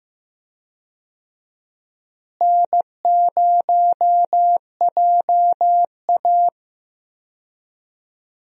Morse code, heard as a keyed tone, about 15 wpm, 700 Hz. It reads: N0JA
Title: USA Call Signs - 4 Characters - Repeated 2x - 15wpm